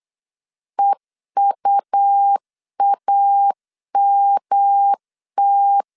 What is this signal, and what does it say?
Morse code, heard as a keyed tone, about 8 wpm, 800 Hz. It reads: EUAMT